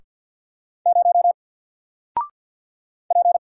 Morse code, heard as a keyed tone, about 48 wpm, 700 Hz. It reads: 0 P
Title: Single Character (Letter:Number:Punctuation:Prosign) Alt 50wpm